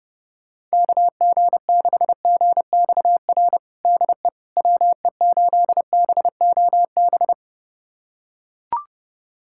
Morse code, heard as a keyed tone, about 30 wpm, 700 Hz. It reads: KG6GXR DE WE8BOB
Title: USA Callsign DE Callsign 30wpm